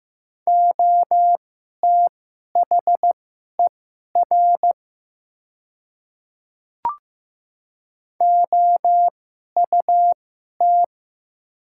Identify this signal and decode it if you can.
Morse code, heard as a keyed tone, about 15 wpm, spaced as for 11 wpm, 700 Hz. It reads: OTHER OUT
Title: 2x Character Spacing - Top 200 Words - 15wpm